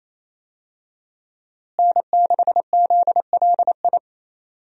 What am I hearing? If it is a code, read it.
Morse code, heard as a keyed tone, about 28 wpm, 700 Hz. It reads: N6ZLI
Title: USA Call Signs - 5 Characters - Repeated 2x - 28wpm